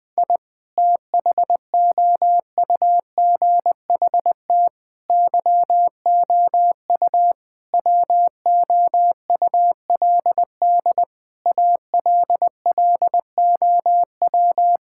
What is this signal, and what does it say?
Morse code, heard as a keyed tone, about 21 wpm, 700 Hz. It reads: I THOUGHT YOU WOULD ALLOW
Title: Sentences from Top 1000 Words - Review Part 5 - 20wpm